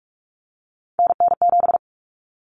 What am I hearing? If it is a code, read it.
Morse code, heard as a keyed tone, about 44 wpm, 700 Hz. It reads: NN7
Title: Common Cut Numbers - 3-Digit 45wpm